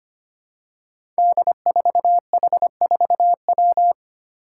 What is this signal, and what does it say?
Morse code, heard as a keyed tone, about 25 wpm, 700 Hz. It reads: D4H4W
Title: Five Challenging Letter-Number Combinations 25wpm